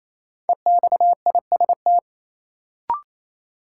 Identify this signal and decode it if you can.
Morse code, heard as a keyed tone, about 28 wpm, 700 Hz. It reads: EXIST
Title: Sentences from Top 1100 Words - New Word Drill Alt - 28wpm